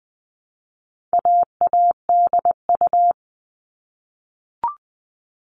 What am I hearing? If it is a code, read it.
Morse code, heard as a keyed tone, about 20 wpm, 700 Hz. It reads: AADU
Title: All Cut Numbers - 4-Digit 20wpm